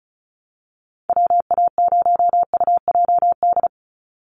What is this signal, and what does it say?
Morse code, heard as a keyed tone, about 34 wpm, 700 Hz. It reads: WA0UJD
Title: USA Call Signs 35wpm